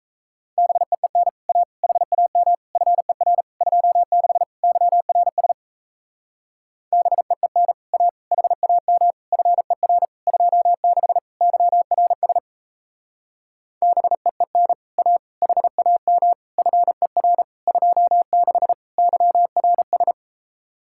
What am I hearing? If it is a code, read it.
Morse code, heard as a keyed tone, about 37 wpm, 700 Hz. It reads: BEEN A HAM FER 26 YRS BEEN A HAM FER 26 YRS BEEN A HAM FER 26 YRS